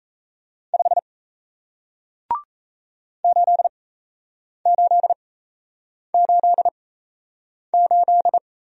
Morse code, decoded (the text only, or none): 5 8888